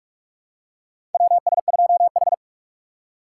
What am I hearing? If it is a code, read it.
Morse code, heard as a keyed tone, about 46 wpm, 700 Hz. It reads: WS2H
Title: USA Call Signs - 4 Characters - 2x Character Spacing - 45wpm